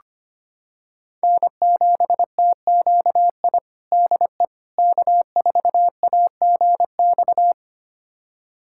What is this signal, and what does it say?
Morse code, heard as a keyed tone, about 25 wpm, 700 Hz. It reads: N7TQI DE K4AGX
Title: USA Callsign DE Callsign 25wpm